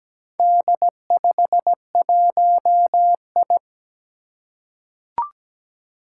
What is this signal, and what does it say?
Morse code, heard as a keyed tone, about 17 wpm, 700 Hz. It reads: D51I
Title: Four Challenging Letter-Number Combinations 17wpm